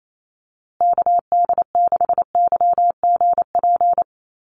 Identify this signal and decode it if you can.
Morse code, heard as a keyed tone, about 28 wpm, 700 Hz. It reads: KD6YGP
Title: USA Call Signs Repeated 3x 28wpm